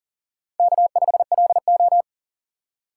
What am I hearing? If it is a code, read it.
Morse code, heard as a keyed tone, about 40 wpm, 700 Hz. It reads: K5LO
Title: USA Call Signs - 4 Characters - 4x Character Spacing - 40wpm